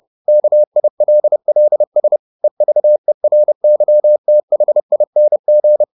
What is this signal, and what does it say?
Morse code, heard as a keyed tone, about 30 wpm, 600 Hz. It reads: KILLS EVERYTHING